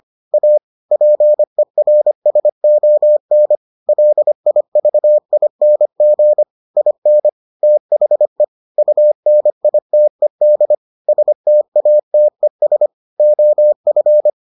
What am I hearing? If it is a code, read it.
Morse code, heard as a keyed tone, about 25 wpm, 600 Hz. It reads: A PERSON LIVING IN THE UNITED STATES OF